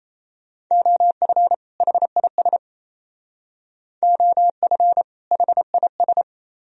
Morse code, heard as a keyed tone, about 35 wpm, 700 Hz. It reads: OF HIS OF HIS
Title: Speed-Racing - Sets of 2 Words 28wpm